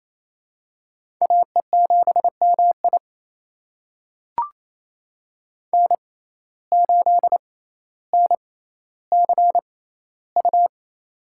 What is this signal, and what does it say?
Morse code, heard as a keyed tone, about 28 wpm, spaced as for 14 wpm, 700 Hz. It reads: AE7MI N8NCU
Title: USA Call Signs - 5 Characters - 6x Character Spacing - 28wpm